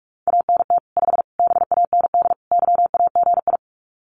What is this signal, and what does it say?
Morse code, heard as a keyed tone, about 44 wpm, 700 Hz. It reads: ANT 5 BAND YAGI
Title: Common QSO Phrases 45wpm